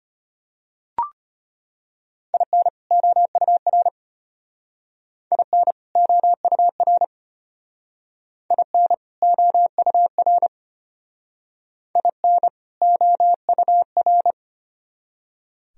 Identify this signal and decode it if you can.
Morse code, heard as a keyed tone, about 32 wpm, 700 Hz. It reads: IN OUR IN OUR IN OUR IN OUR